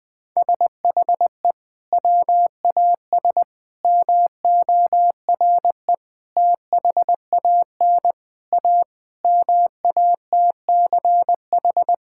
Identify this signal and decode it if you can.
Morse code, heard as a keyed tone, about 20 wpm, 700 Hz. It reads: SHE WAS MORE THAN A MATCH
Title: Sentences from Top 1600 Words 20wpm